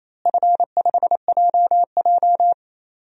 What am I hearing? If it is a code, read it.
Morse code, heard as a keyed tone, about 28 wpm, 700 Hz. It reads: F5JJ
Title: DXCC-100 Call Signs - 4 Characters - Repeated 3x - 28wpm